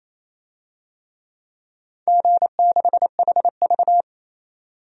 Morse code, heard as a keyed tone, about 28 wpm, 700 Hz. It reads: G6HV
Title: DXCC-100 Call Signs - 4 Characters - 28wpm